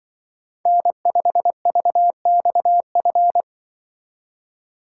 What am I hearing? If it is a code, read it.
Morse code, heard as a keyed tone, about 24 wpm, 700 Hz. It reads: N5VXF
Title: Speed Racing - USA Call Signs 20wpm